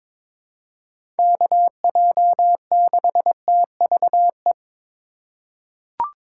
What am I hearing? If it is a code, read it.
Morse code, heard as a keyed tone, about 22 wpm, 700 Hz. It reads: KJ6TVE
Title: USA Call Signs - 6 Characters - 4x Character Spacing - 22wpm